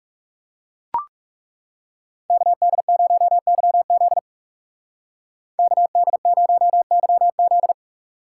Morse code, heard as a keyed tone, about 37 wpm, 700 Hz. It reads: KD0YZ KD0YZ
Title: Speed Racing - USA 2x2 Call Signs 30wpm